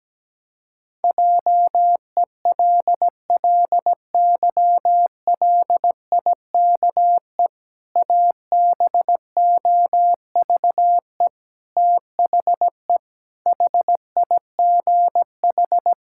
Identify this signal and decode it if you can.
Morse code, heard as a keyed tone, about 17 wpm, 700 Hz. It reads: JELLYLIKE ABOVE THE HIGH
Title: Pangram Challenge v2 - 17wpm